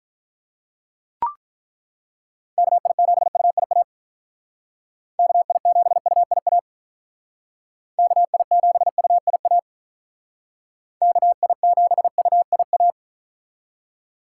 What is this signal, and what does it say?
Morse code, heard as a keyed tone, about 42 wpm, 700 Hz. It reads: KI7UIA KI7UIA KI7UIA KI7UIA
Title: Speed Racing - USA Call Signs 35wpm